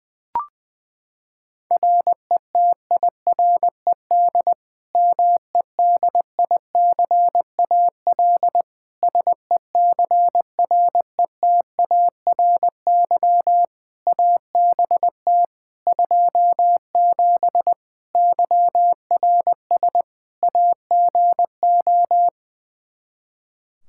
Morse code, heard as a keyed tone, about 20 wpm, 700 Hz. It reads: RETIRED MEDICAL SECRETARY ABT 27 YRS AGO
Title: Practice QSO - Exchange 3 - More Info - 20wpm